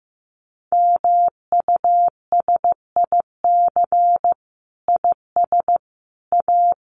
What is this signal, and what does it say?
Morse code, heard as a keyed tone, about 15 wpm, 700 Hz. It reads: MUSIC IS A
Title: Sentences from Top 300 Words 15wpm